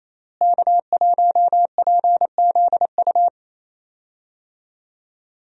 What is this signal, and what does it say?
Morse code, heard as a keyed tone, about 27 wpm, 700 Hz. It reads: K1PZU
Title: USA Call Signs - 5 Characters - Repeated 2x - 28wpm